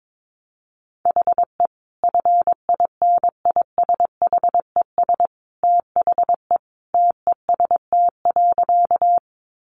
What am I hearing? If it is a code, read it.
Morse code, heard as a keyed tone, about 22 wpm, 700 Hz. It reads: HE FINISHES THE TEST.
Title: Additional Easy Phrases 22wpm